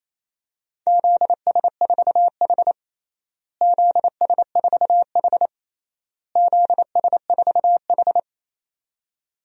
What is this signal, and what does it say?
Morse code, heard as a keyed tone, about 28 wpm, 700 Hz. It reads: ZS4H ZS4H ZS4H
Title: DXCC-100 Call Signs - 4 Characters - Repeated 3x - 28wpm